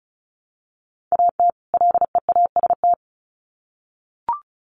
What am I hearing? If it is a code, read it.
Morse code, heard as a keyed tone, about 34 wpm, 700 Hz. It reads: AT LEAST
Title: Sets of 2 Words 35wpm